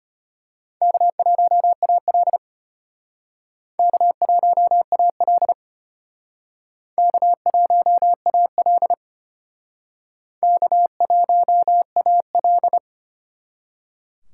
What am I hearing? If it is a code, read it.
Morse code, heard as a keyed tone, about 31 wpm, 700 Hz. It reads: K1AL K1AL K1AL K1AL
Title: Speed Racing - USA 1x2 Call Signs 25wpm